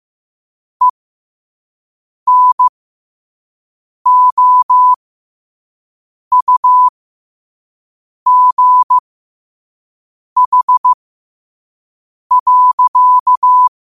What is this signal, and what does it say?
Morse code, heard as a keyed tone, about 15 wpm, spaced as for 5 wpm, 1000 Hz. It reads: ENOUGH.
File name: qotdfiles / 5wpm